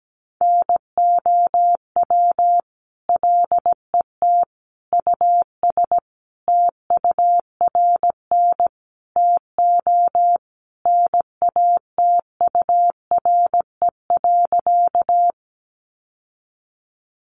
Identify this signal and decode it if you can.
Morse code, heard as a keyed tone, about 17 wpm, 700 Hz. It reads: NOW LET US TURN TO NATURE.
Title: Sentences from Top 600 Words 17wpm